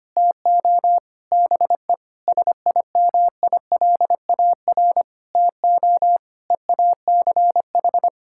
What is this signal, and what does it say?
Morse code, heard as a keyed tone, about 25 wpm, 700 Hz. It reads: TO BE SIMILAR TO EACH
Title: Sentences from Top 1000 Words - Review Part 5 - 25wpm